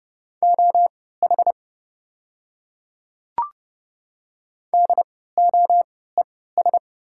Morse code, heard as a keyed tone, about 30 wpm, spaced as for 17 wpm, 690 Hz. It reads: OH DOES
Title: 3x Character Spacing - Top 500 Words - 30wpm